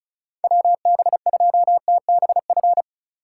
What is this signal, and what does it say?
Morse code, heard as a keyed tone, about 36 wpm, 700 Hz. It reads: WB2TBF
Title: USA Call Signs - 6 Characters - 3x Character Spacing - 35wpm